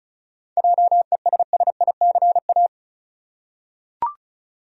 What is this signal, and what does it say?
Morse code, heard as a keyed tone, about 34 wpm, 700 Hz. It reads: JESSICA
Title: USA Common Names - 35wpm